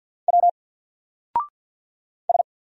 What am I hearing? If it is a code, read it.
Morse code, heard as a keyed tone, about 50 wpm, 700 Hz. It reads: W S
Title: Single Character (Letter:Number:Punctuation:Prosign) 50wpm